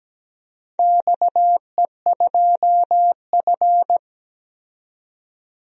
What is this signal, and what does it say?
Morse code, heard as a keyed tone, about 17 wpm, 700 Hz. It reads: XE2F
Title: DXCC-100 Call Signs - 4 Characters - 17wpm